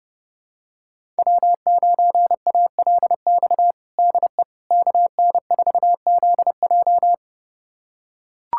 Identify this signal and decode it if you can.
Morse code, heard as a keyed tone, about 30 wpm, 700 Hz. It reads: W9ALX DE KN4ZJ
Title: USA Callsign DE Callsign - 5 Characters - 30wpm